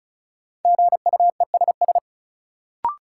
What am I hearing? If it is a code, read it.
Morse code, heard as a keyed tone, about 34 wpm, 700 Hz. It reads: GUESS
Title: Sentences from Top 1500 Words - New Word Drill Alt - 35wpm